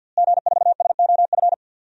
Morse code, heard as a keyed tone, about 46 wpm, 700 Hz. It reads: G4SOF
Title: DXCC-100 Call Signs - 5 Characters - Repeated 2x - 50wpm